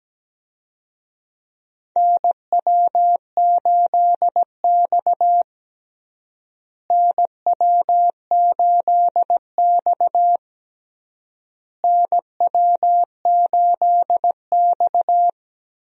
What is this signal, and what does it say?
Morse code, heard as a keyed tone, about 17 wpm, 700 Hz. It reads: NW8X NW8X NW8X
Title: USA Call Signs - 4 Characters - Repeated 3x - 17wpm